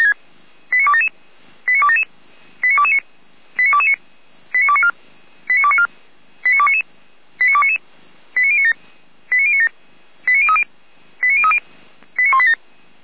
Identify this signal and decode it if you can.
Five-tone selective call (ZVEI), heard as quick runs of five tones, tones of about 70 ms. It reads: 89277 89277 89229 89229 89294 89294 89299 89299 89097 89097 89030 89030 89187
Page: /455/SDR/ZVEI/89310/